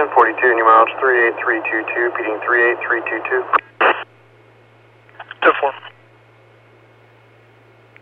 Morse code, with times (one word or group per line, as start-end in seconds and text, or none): none